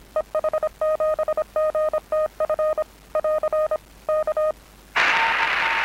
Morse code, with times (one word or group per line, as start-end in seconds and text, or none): 0.2-2.8 EH7GTF
3.2-3.8 <AR>
4.1-4.5 K